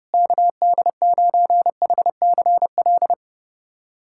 Morse code, heard as a keyed tone, about 30 wpm, 700 Hz. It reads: KD9HCL
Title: USA Callsign DE Callsign - 6 Characters - 30wpm